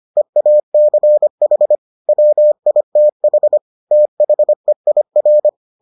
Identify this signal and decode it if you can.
Morse code, heard as a keyed tone, about 25 wpm, 600 Hz. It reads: EACH WITH THEIR